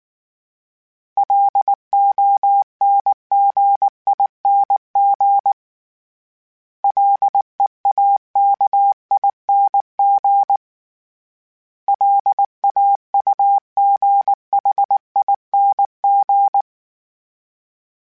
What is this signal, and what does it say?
Morse code, heard as a keyed tone, about 19 wpm, 800 Hz. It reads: LONGING LEAKING LAUGHING
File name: Special INGs / Special INGs-mp3s / Sngl Ing @18